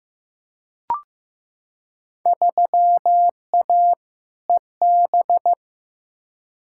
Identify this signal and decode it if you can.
Morse code, heard as a keyed tone, about 15 wpm, 700 Hz. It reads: 3A EB
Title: Field Day Exchanges Encore - 15wpm